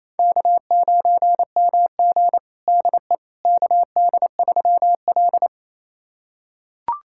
Morse code, heard as a keyed tone, about 28 wpm, 700 Hz. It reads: K9MG DE KD3L
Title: USA Callsign DE Callsign - 4 Characters - 28wpm